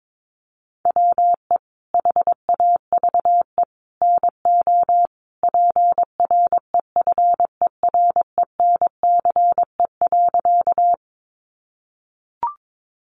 Morse code, heard as a keyed tone, about 22 wpm, 700 Hz. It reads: WE HAVE NO PREFERENCE.
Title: Sentences from Top 1500 Words 22wpm